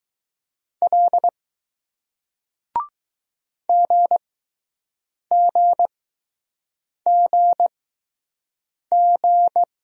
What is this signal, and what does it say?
Morse code, heard as a keyed tone, about 20 wpm, spaced as for 6 wpm, 700 Hz. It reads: L GGGG